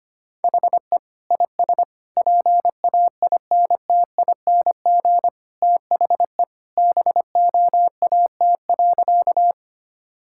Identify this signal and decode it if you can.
Morse code, heard as a keyed tone, about 25 wpm, 700 Hz. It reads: HE IS PAINTING THE BOAT.
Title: Sentences from Top 1000 Words - Review Part 6 - 25wpm